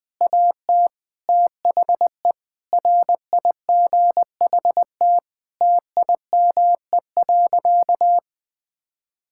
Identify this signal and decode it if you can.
Morse code, heard as a keyed tone, about 20 wpm, 700 Hz. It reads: AT THE RIGHT TIME.
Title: Sentences with 7-Word Phrases - Spoken Beginning - 20wpm